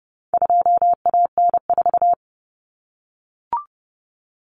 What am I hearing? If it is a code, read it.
Morse code, heard as a keyed tone, about 30 wpm, 700 Hz. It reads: 2AN4
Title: Common Cut Numbers - 4-Digit 30wpm